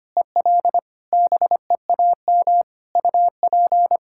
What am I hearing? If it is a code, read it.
Morse code, heard as a keyed tone, about 25 wpm, 700 Hz. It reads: EL BEAM UP
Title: Practice QSO - Exchange 2 - Info - 25wpm